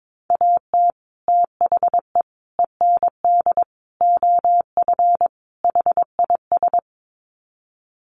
Morse code, heard as a keyed tone, about 22 wpm, 700 Hz. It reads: AT THE END OF HIS
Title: Sets of 5 Words Encore 22wpm